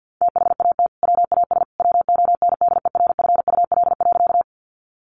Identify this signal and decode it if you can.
Morse code, heard as a keyed tone, about 50 wpm, 700 Hz. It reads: THAT WAS WONDERFUL.
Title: Sentences from Top 1100 Words 50wpm